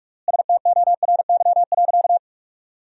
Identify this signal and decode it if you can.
Morse code, heard as a keyed tone, about 46 wpm, 700 Hz. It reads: STORY.
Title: 3x Word-Spacing - Sentences from Top 300 Words - 45wpm